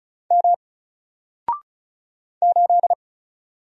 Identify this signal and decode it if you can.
Morse code, heard as a keyed tone, about 36 wpm, 700 Hz. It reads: M 8
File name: Single-Letter-Number-35wpm